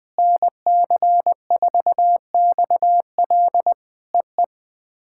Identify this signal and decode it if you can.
Morse code, heard as a keyed tone, about 20 wpm, 700 Hz. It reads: NC4XL EE
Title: POTA QSO Practice - 20wpm